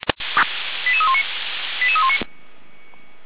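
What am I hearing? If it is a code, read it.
Five-tone selective call (ZVEI), heard as quick runs of five tones, tones of about 70 ms. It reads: 88319 88319
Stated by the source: ZVEI five-tone call received with an SDR.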